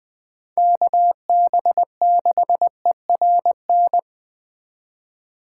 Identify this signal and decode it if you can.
Morse code, heard as a keyed tone, about 20 wpm, 700 Hz. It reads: KB6ERN
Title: Speed Racing - USA 2x3 Call Signs 17wpm